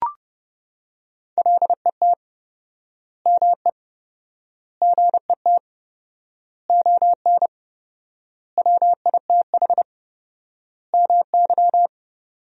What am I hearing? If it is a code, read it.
Morse code, heard as a keyed tone, about 30 wpm, 700 Hz. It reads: LET ME GET ON WITH MY
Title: 4x Word-Spacing - Sentences from Top 300 Words - 30wpm